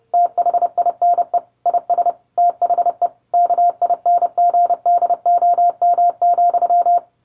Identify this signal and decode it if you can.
Morse code, heard as a keyed tone, about 30 wpm, 700 Hz. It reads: THINE IS THE KINGDOM,